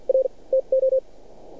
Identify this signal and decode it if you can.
Morse code, heard as a keyed tone, about 50 wpm, spaced as for 39 wpm, 530 Hz. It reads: R TO
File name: archived / CRSnet